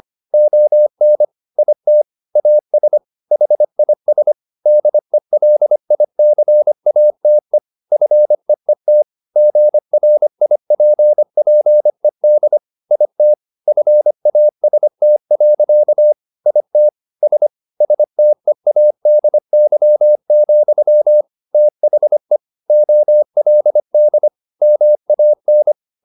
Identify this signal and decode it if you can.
Morse code, heard as a keyed tone, about 25 wpm, 600 Hz. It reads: ON IT AS HIS DELICATE FEET GRIPPED IT FAST. IT S STEADY, THE OLD MAN
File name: the old man and the sea part 1 25wpm